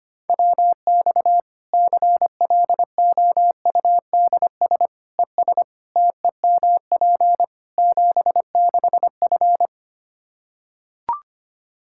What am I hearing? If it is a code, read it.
Morse code, heard as a keyed tone, about 25 wpm, 700 Hz. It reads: WX CLOUDS ES TEMP 76F
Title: K1USN SST QSO Practice 25wpm